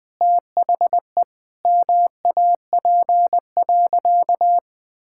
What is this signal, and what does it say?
Morse code, heard as a keyed tone, about 20 wpm, 700 Hz. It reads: THE MAP.
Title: Sentences with 8-Word Phrases - Spoken Beginning - 20wpm